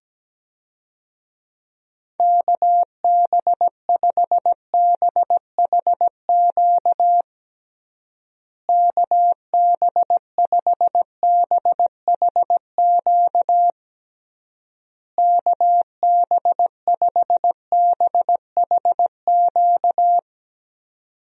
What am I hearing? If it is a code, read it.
Morse code, heard as a keyed tone, about 17 wpm, 700 Hz. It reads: KB5BHQ KB5BHQ KB5BHQ
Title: USA Call Signs - 6 Characters - Repeated 3x - 17wpm